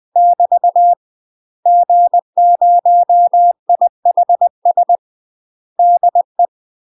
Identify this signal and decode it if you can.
Morse code, heard as a keyed tone, about 20 wpm, 700 Hz. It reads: <BT> G0IHS DE
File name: UK MASTER QSOs TRAINING / 20WPM / UK MASTER QSOs 0035